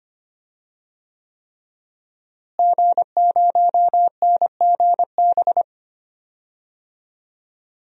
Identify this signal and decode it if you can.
Morse code, heard as a keyed tone, about 25 wpm, 700 Hz. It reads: G0NGB
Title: DXCC-100 Call Signs - 5 Characters - Repeated 3x - 25wpm